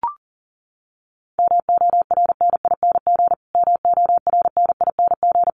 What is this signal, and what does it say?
Morse code, heard as a keyed tone, about 40 wpm, 700 Hz. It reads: MORNING MORNING
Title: Top 500 Words Repeated 40wpm